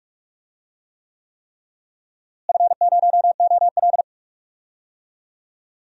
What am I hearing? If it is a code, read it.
Morse code, heard as a keyed tone, about 46 wpm, 700 Hz. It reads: F0OL
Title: DXCC-100 Call Signs - 4 Characters - Repeated 3x - 45wpm